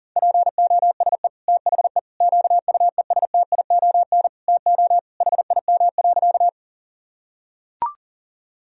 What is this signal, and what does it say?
Morse code, heard as a keyed tone, about 40 wpm, 700 Hz. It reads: POSE THE QUESTION TO HIM.
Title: Sentences from Top 1000 Words - Review Part 2 - 40wpm